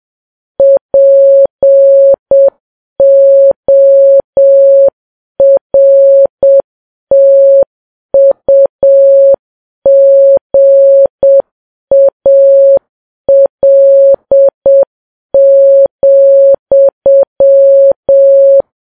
Morse code, heard as a keyed tone, about 7 wpm, 550 Hz. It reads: PORTUGAL,